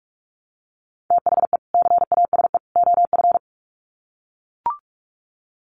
Morse code, heard as a keyed tone, about 44 wpm, 700 Hz. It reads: THE CASE OF
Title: Sets of 3 Words 45wpm